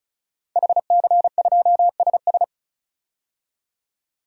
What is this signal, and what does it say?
Morse code, heard as a keyed tone, about 35 wpm, 700 Hz. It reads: HC2SS
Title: DXCC-100 Call Signs - 5 Characters - 35wpm